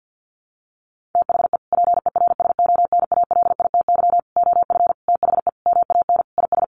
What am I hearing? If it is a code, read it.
Morse code, heard as a keyed tone, about 48 wpm, 700 Hz. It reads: THE PERSONALITY OF THE MAN IS
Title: Sentences from Top 1400 Words 50wpm